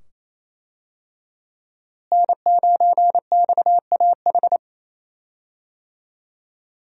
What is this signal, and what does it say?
Morse code, heard as a keyed tone, about 28 wpm, 700 Hz. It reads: N9XAH
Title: USA Call Signs - 5 Characters - Repeated 2x - 28wpm